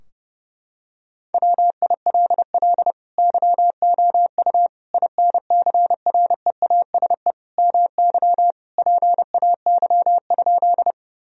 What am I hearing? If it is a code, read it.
Morse code, heard as a keyed tone, about 30 wpm, 700 Hz. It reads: WILL YOU INCREASE MY PAY?
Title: Sentences from Top 1000 Words - Review Part 4 - 30wpm